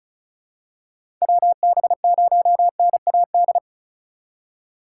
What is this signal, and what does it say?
Morse code, heard as a keyed tone, about 36 wpm, 700 Hz. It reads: WB0NAD
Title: USA Call Signs - 6 Characters - 3x Character Spacing - 35wpm